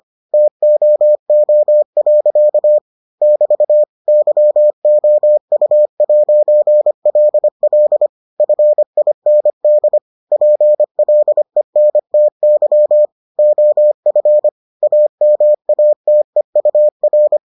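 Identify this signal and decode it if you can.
Morse code, heard as a keyed tone, about 25 wpm, 600 Hz. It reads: TOO. <BT> YOU'LL FIND PLENTY OF AMATEUR
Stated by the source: Morse edition of the Foundations of Amateur Radio episodes.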